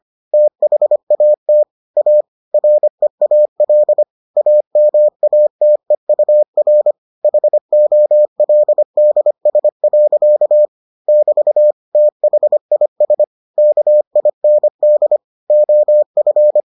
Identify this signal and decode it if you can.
Morse code, heard as a keyed tone, about 25 wpm, 600 Hz. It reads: THAT A REAL AMATEUR HOLDS. <BT> THIS KIND OF